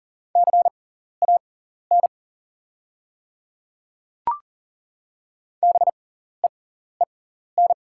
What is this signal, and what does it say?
Morse code, heard as a keyed tone, about 40 wpm, spaced as for 14 wpm, 700 Hz. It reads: CAN BEEN
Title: 6x Character Spacing - Top 100 Words - 40wpm